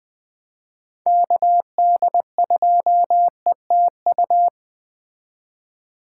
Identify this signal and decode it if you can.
Morse code, heard as a keyed tone, about 20 wpm, 700 Hz. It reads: KD2ETU
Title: Speed Racing - USA 2x3 Call Signs 17wpm